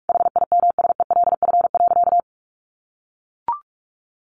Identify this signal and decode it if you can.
Morse code, heard as a keyed tone, about 44 wpm, 700 Hz. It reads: HIMSELF.
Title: Additional Easy Phrases 45wpm